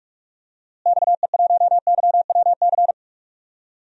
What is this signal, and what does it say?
Morse code, heard as a keyed tone, about 46 wpm, 700 Hz. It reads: XE1YWC
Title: DXCC-100 Call Signs - 6 Characters - Repeated 3x - 45wpm